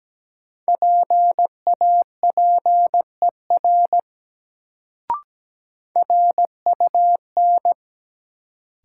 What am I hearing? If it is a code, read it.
Morse code, heard as a keyed tone, about 17 wpm, 700 Hz. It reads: PAPER RUN
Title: Sentences from Top 300 Words - New Word Drill - 17wpm